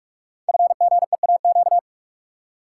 Morse code, heard as a keyed tone, about 44 wpm, 700 Hz. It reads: FGEAQ
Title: Code Groups - 5 Letters - 45wpm